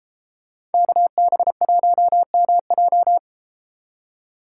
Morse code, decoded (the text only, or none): KB1MJ